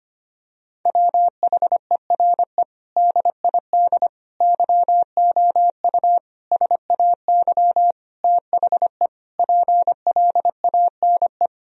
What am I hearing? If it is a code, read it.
Morse code, heard as a keyed tone, about 25 wpm, 700 Hz. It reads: WHERE DID YOU SAY THE PLANE
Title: Sentences from Top 1000 Words - Review Part 3 - 25wpm